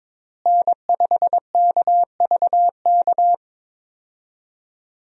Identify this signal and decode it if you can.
Morse code, heard as a keyed tone, about 22 wpm, 700 Hz. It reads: N5KVK
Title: USA Call Signs - 5 Characters - Repeated 3x - 22wpm